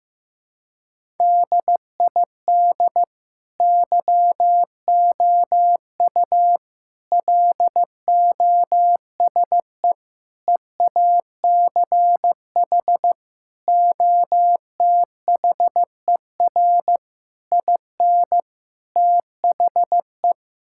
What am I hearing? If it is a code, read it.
Morse code, heard as a keyed tone, about 15 wpm, 700 Hz. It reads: DID YOU LOSE EACH OTHER IN THE
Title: Sentences from Top 1600 Words 15wpm